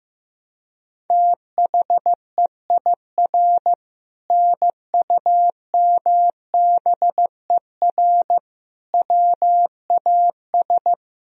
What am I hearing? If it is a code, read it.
Morse code, heard as a keyed tone, about 15 wpm, 700 Hz. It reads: THEIR NUMBER WAS
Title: Sentences from Top 1000 Words - Review Part 1 - 15wpm